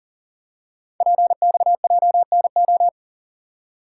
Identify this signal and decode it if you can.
Morse code, heard as a keyed tone, about 38 wpm, 700 Hz. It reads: PXJNO
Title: Code Groups - 5 Letters - 40wpm